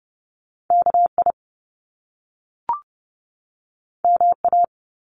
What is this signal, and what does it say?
Morse code, heard as a keyed tone, about 30 wpm, 700 Hz. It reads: KI MA
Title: Top Two Letter Combinations in Words 30wpm